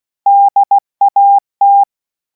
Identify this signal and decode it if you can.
Morse code, heard as a keyed tone, about 16 wpm, 800 Hz. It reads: DAT